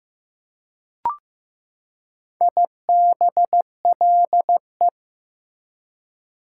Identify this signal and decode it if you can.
Morse code, heard as a keyed tone, about 15 wpm, 700 Hz. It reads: IBLE
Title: Prefixes and Suffixes 15wpm